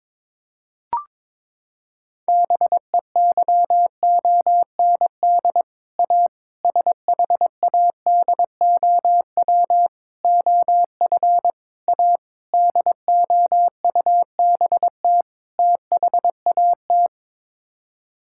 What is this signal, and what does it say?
Morse code, heard as a keyed tone, about 23 wpm, 700 Hz. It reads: BEYOND A SHADOW OF A DOUBT THAT
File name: practice / 22wpm / Sets of 7 Words - 22wpm